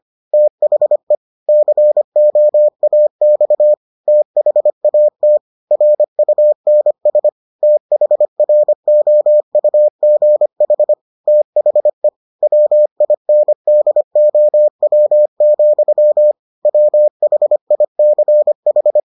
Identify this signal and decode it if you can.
Morse code, heard as a keyed tone, about 25 wpm, 600 Hz. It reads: THE COAX THAT RUNS THROUGH THE WINDOW, WHICH